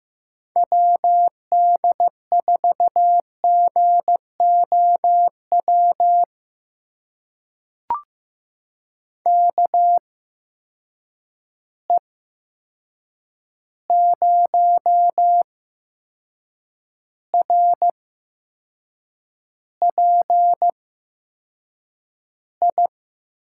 Morse code, heard as a keyed tone, about 15 wpm, spaced as for 6 wpm, 700 Hz. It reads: WD4GOW KE0RPI
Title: USA Call Signs - 6 Characters - 8x Character Spacing - 15wpm